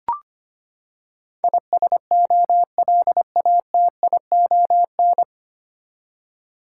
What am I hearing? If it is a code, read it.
Morse code, heard as a keyed tone, about 25 wpm, 700 Hz. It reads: ISOLATION
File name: Nine Letter Words - 25wpm